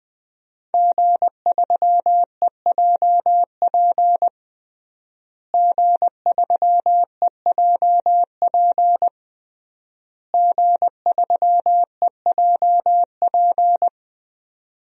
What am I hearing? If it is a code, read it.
Morse code, heard as a keyed tone, about 21 wpm, 700 Hz. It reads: G3EJP G3EJP G3EJP
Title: DXCC-100 Call Signs - 5 Characters - Repeated 3x - 20wpm